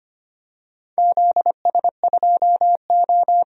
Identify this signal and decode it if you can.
Morse code, heard as a keyed tone, about 25 wpm, 700 Hz. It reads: ZS2O
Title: DXCC-100 Call Signs - 4 Characters - 25wpm